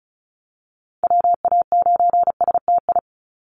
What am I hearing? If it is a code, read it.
Morse code, heard as a keyed tone, about 34 wpm, 700 Hz. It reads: WA9STI
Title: NA SOTA Activator Callsigns 35wpm